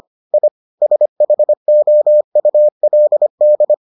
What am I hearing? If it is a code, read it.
Morse code, heard as a keyed tone, about 25 wpm, 600 Hz. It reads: I SHOULD